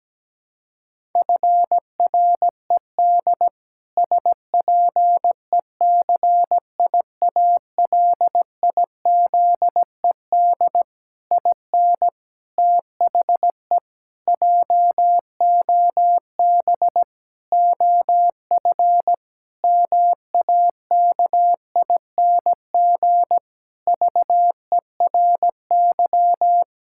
Morse code, heard as a keyed tone, about 17 wpm, 700 Hz. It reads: FRED SPECIALIZED IN THE JOB OF MAKING VERY
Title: Pangram Challenge v2 - 17wpm